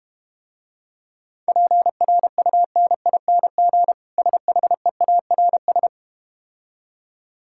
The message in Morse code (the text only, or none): PRUNING SHEARS